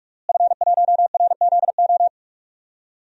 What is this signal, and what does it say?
Morse code, heard as a keyed tone, about 42 wpm, 700 Hz. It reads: F1RZO
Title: DXCC-100 Call Signs - 5 Characters - Repeated 2x - 45wpm